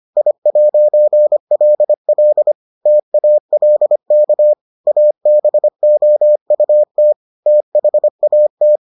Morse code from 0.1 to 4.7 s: I'LL TALK

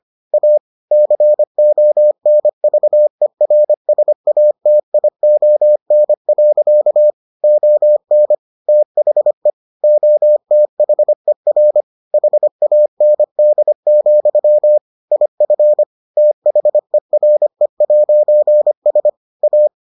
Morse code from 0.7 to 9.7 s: CONVERSATION. ON THE

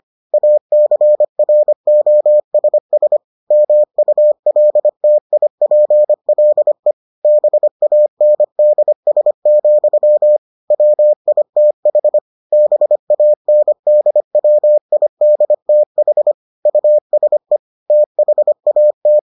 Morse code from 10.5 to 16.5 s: WITH BANDWIDTH

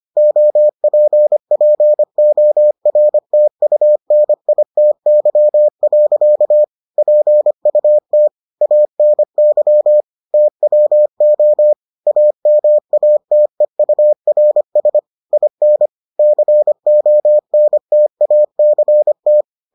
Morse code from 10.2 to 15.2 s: TWO AMATEURS

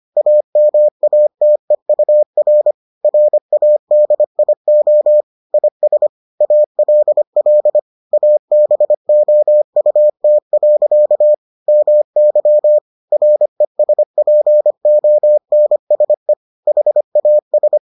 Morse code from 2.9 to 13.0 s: RADIO IS ALL ABOUT. MY

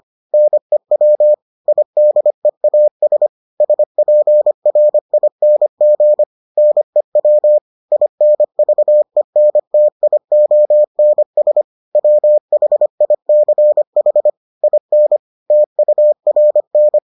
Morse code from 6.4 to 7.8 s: NEW

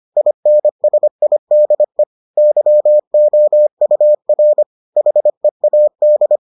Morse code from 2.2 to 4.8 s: YOUR

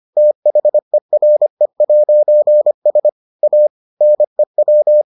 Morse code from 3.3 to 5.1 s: A NEW

